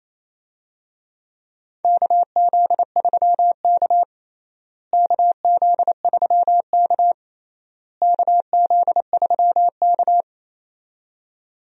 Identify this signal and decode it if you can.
Morse code, heard as a keyed tone, about 28 wpm, 700 Hz. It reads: KZ3K KZ3K KZ3K
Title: USA Call Signs - 4 Characters - Repeated 3x - 28wpm